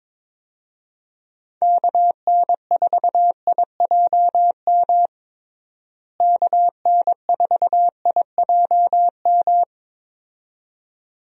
Morse code, heard as a keyed tone, about 22 wpm, 700 Hz. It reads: KN4IJM KN4IJM
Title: USA Call Signs - 6 Characters - Repeated 2x - 22wpm